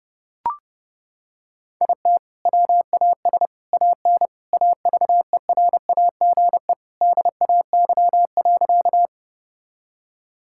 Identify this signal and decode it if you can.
Morse code, heard as a keyed tone, about 32 wpm, 700 Hz. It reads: IT WAS AN AVERAGE DAY.
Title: Sentences from Top 1000 Words - Review Part 6 - 30wpm